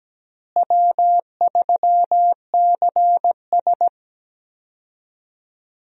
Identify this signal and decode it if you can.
Morse code, heard as a keyed tone, about 17 wpm, 700 Hz. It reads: W3CS
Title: USA Call Signs - 4 Characters - Repeated 2x - 17wpm